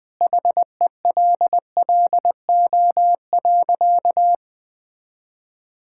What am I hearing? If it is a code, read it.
Morse code, heard as a keyed tone, about 21 wpm, 700 Hz. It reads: HELLO.
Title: Sentences with 2-Word Phrases - Spoken Beginning - 20wpm